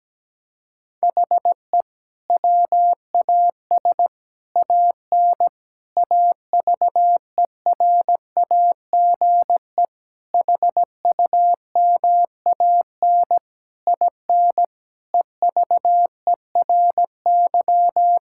Morse code, heard as a keyed tone, about 17 wpm, 700 Hz. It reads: HE WAS AN AVERAGE HUMAN IN EVERY
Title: Sentences from Top 1000 Words - Review Part 6 - 17wpm